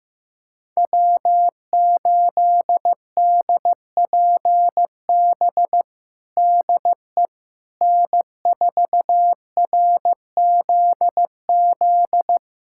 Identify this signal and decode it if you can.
Morse code, heard as a keyed tone, about 15 wpm, 700 Hz. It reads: W8DPB DE N4RZZ
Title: USA Callsign DE Callsign - 5 Characters - 15wpm